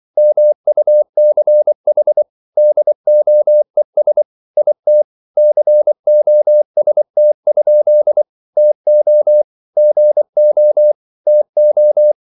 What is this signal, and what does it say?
Morse code, heard as a keyed tone, about 24 wpm, 600 Hz. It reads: MUCH DOES IT COST? TO GO TO